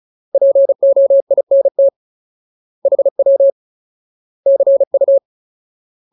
Morse code, heard as a keyed tone, about 34 wpm, 550 Hz. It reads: POINT HW CU
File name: Ultimate Word List / 35WPM Ultimate